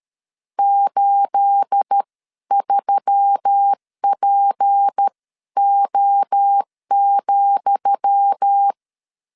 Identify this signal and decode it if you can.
Morse code, heard as a keyed tone, about 13 wpm, spaced as for 11 wpm, 800 Hz. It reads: 83PO,
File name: LEZIONI / 10 SIGNS / 10D COMA 12wpm 2544